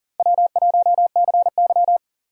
Morse code, heard as a keyed tone, about 40 wpm, 700 Hz. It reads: W1CY
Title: USA Call Signs - 4 Characters - 2x Character Spacing - 40wpm